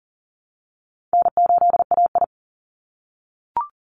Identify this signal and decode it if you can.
Morse code, heard as a keyed tone, about 41 wpm, 700 Hz. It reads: N8AI
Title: Easy Call Signs - 40wpm